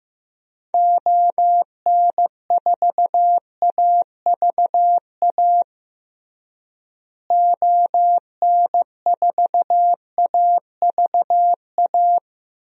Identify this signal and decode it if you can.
Morse code, heard as a keyed tone, about 15 wpm, 700 Hz. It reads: ON4AVA ON4AVA
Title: DXCC-100 Call Signs - 6 Characters - Repeated 3x - 15wpm